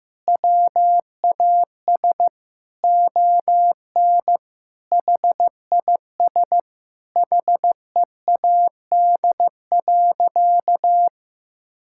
Morse code, heard as a keyed tone, about 16 wpm, 700 Hz. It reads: WAS ON HIS HEAD.
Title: Sentences from Top 1600 Words 15wpm